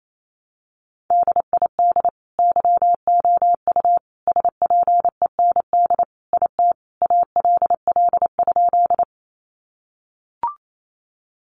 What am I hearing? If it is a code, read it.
Morse code, heard as a keyed tone, about 28 wpm, 700 Hz. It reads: DID YOU SPEND IT ALL?
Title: Sentences from Top 800 Words 28wpm